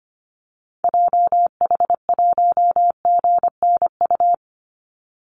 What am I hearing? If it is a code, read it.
Morse code, heard as a keyed tone, about 25 wpm, 700 Hz. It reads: JH1GNU
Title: Call Signs - 25wpm